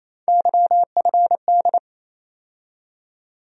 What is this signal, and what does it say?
Morse code, heard as a keyed tone, about 27 wpm, 700 Hz. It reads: YFD
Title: USA Call Sign Postfixes - 3 Letters 28wpm